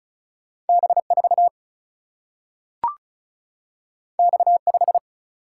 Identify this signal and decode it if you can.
Morse code, heard as a keyed tone, about 36 wpm, 700 Hz. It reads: B4 X5
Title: Single-Letter Single-Number 35wpm